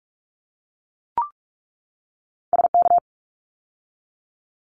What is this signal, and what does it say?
Morse code, heard as a keyed tone, about 44 wpm, 700 Hz. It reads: SK
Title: ARRL Sections 45wpm